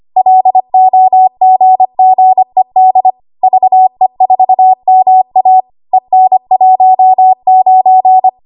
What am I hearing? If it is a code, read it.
Morse code, heard as a keyed tone, about 25 wpm, 750 Hz. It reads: LOGGED VE4MA EN19
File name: Morse / 25WPM